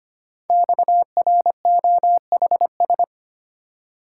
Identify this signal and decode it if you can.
Morse code, heard as a keyed tone, about 25 wpm, 700 Hz. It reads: XROHS
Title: Code Groups - 5 Letters - 25wpm